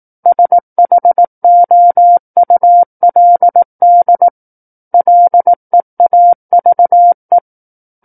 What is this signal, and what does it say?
Morse code, heard as a keyed tone, about 18 wpm, 700 Hz. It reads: SHOULD LEAVE
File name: 18 WPM